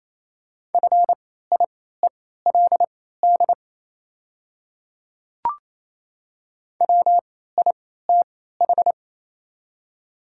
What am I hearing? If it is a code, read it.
Morse code, heard as a keyed tone, about 28 wpm, spaced as for 16 wpm, 690 Hz. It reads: FIELD WITH
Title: 3x Character Spacing - Top 500 Words - 28wpm